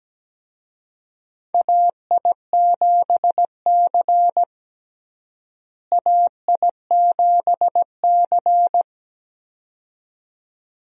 Morse code, heard as a keyed tone, about 17 wpm, 700 Hz. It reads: AI7C AI7C
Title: USA Call Signs - 4 Characters - Repeated 2x - 17wpm